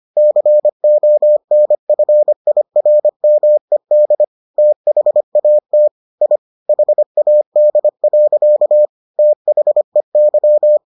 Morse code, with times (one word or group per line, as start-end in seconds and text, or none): 0.2-4.2 CONFIRMED
4.6-5.9 THAT
6.2-6.4 I
6.7-8.9 HAD.
9.2-10.8 THEY